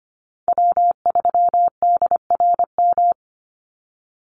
Morse code, heard as a keyed tone, about 25 wpm, 700 Hz. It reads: W3DRM
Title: USA Call Signs 25wpm